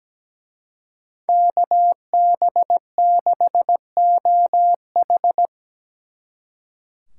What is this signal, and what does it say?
Morse code, heard as a keyed tone, about 20 wpm, 700 Hz. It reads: KB6OH
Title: Speed Racing - USA 2x2 Call Signs 17wpm